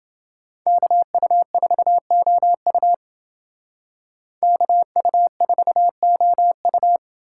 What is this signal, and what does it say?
Morse code, heard as a keyed tone, about 29 wpm, 700 Hz. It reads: KU4OU KU4OU